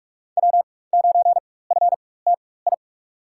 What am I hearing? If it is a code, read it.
Morse code, heard as a keyed tone, about 43 wpm, spaced as for 21 wpm, 700 Hz. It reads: W9FTI
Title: USA Call Signs - 5 Characters - 4x Character Spacing - 45wpm